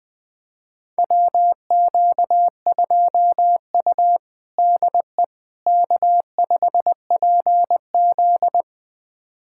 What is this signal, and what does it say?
Morse code, heard as a keyed tone, about 20 wpm, 700 Hz. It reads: WQ2U DE K5PZ
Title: USA Callsign DE Callsign - 4 Characters - 20wpm